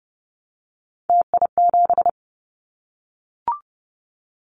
Morse code, heard as a keyed tone, about 30 wpm, 700 Hz. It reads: TI7
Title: Callsign Prefixes for Top 100 DXCC - 30wpm